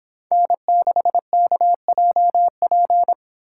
Morse code, heard as a keyed tone, about 25 wpm, 700 Hz. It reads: N6KJP